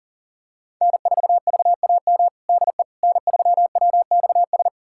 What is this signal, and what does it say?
Morse code, heard as a keyed tone, about 37 wpm, 700 Hz. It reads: N4VAM DE N3WXS
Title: USA Callsign DE Callsign - 5 Characters - 40wpm